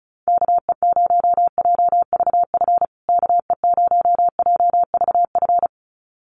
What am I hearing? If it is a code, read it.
Morse code, heard as a keyed tone, about 34 wpm, 700 Hz. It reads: KE0JVF KE0JVF
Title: USA Call Signs Repeated 2x 35wpm